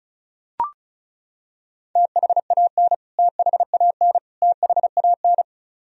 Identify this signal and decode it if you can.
Morse code, heard as a keyed tone, about 36 wpm, 700 Hz. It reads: THAN THAN THAN
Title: Top 100 Words Repeated 35wpm